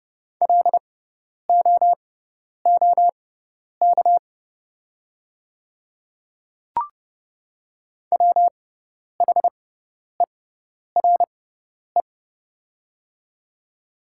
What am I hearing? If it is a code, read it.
Morse code, heard as a keyed tone, about 31 wpm, spaced as for 10 wpm, 690 Hz. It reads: LOOK WHERE
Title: 6x Character Spacing - Top 300 Words - 30wpm